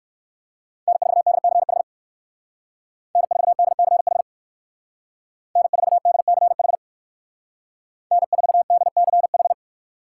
Morse code, heard as a keyed tone, about 63 wpm, 700 Hz. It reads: N4DCH N4DCH N4DCH N4DCH